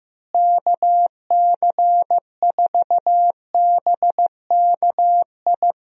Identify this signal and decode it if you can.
Morse code, heard as a keyed tone, about 15 wpm, 700 Hz. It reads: KC4BKI
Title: USA Call Signs - 6 Characters - Repeated 3x - 15wpm